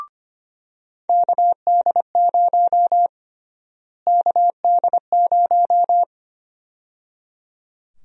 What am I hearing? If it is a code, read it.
Morse code, heard as a keyed tone, about 25 wpm, 700 Hz. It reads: KD0 KD0
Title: USA Call Sign Prefixes - Repeated 2x - 25wpm